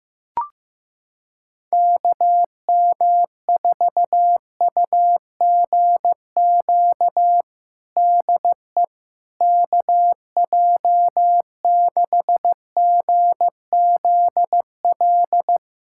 Morse code, heard as a keyed tone, about 15 wpm, 700 Hz. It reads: KM4UGQ DE KJ6GZL
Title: USA Callsign DE Callsign 15wpm